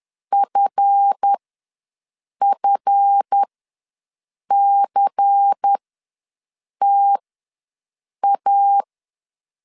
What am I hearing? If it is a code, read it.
Morse code, heard as a keyed tone, about 10 wpm, spaced as for 6 wpm, 800 Hz. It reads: FFCTA
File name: LEZIONI / 7 GF9Z / 7 GF ws 10wpm 3918